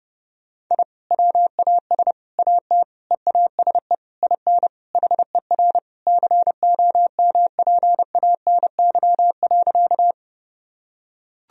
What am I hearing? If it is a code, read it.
Morse code, heard as a keyed tone, about 32 wpm, 700 Hz. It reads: I WAS AT EASE IN HER COMPANY.
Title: Sentences from Top 1000 Words - Review Part 1 - 30wpm